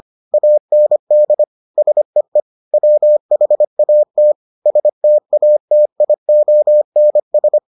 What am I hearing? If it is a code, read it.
Morse code, heard as a keyed tone, about 25 wpm, 600 Hz. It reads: AND SEE WHAT STATIONS